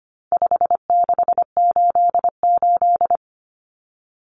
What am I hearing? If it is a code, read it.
Morse code, heard as a keyed tone, about 25 wpm, 700 Hz. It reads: 5688
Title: Common Cut Numbers - 4-Digit 25wpm